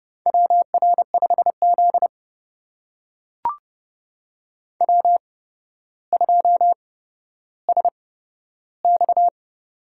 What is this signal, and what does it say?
Morse code, heard as a keyed tone, about 30 wpm, spaced as for 13 wpm, 700 Hz. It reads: WR5Z W2SX
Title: USA Call Signs - 4 Characters - 8x Character Spacing - 30wpm